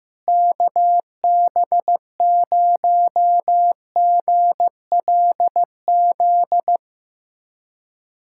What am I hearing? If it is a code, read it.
Morse code, heard as a keyed tone, about 16 wpm, 700 Hz. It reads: KB0GLZ
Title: USA Callsign DE Callsign - 6 Characters - 15wpm